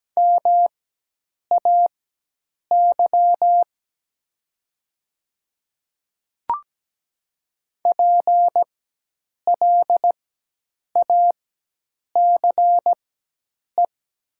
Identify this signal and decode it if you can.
Morse code, heard as a keyed tone, about 17 wpm, spaced as for 8 wpm, 700 Hz. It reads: MAY PLACE
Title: 4x Character Spacing - Top 300 Words - 17wpm